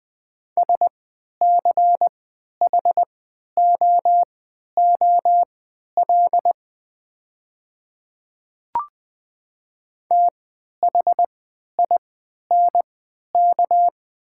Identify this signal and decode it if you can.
Morse code, heard as a keyed tone, about 20 wpm, spaced as for 11 wpm, 700 Hz. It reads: SCHOOL THINK
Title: 3x Character Spacing - Top 500 Words - 20wpm